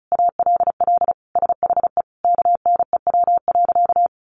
Morse code, heard as a keyed tone, about 34 wpm, 700 Hz. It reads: ALL SHE KNEW.
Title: Sentences from Top 1300 Words 35wpm